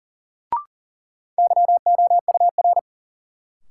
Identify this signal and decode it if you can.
Morse code, heard as a keyed tone, about 40 wpm, 700 Hz. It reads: YOUR
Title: Sets of 2 Words - New Word Drill - 40wpm